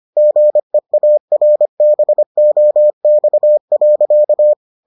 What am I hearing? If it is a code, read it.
Morse code, heard as a keyed tone, about 25 wpm, 600 Hz. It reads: GEARBOX.